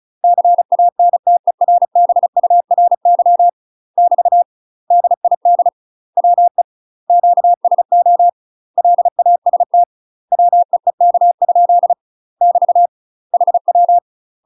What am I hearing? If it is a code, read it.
Morse code, heard as a keyed tone, about 36 wpm, 700 Hz. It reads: CANTERBURY <BT> DID WE QSO LAST WEEK? <BT> HW
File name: UK MASTER QSOs TRAINING / 35WPM / UK MASTER QSOs 0031